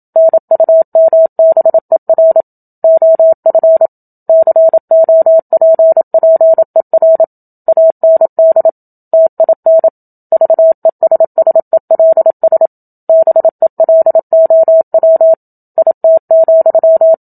Morse code, heard as a keyed tone, about 27 wpm, 650 Hz. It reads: NUMBER OF COPPER AND TIN VESSELS BELOW IT,
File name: Book Two / 01-Chapter 1 - 27 WPM